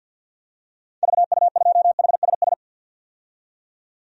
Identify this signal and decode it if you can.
Morse code, heard as a keyed tone, about 46 wpm, 700 Hz. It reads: VU2HSS
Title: DXCC-100 Call Signs - 6 Characters - 50wpm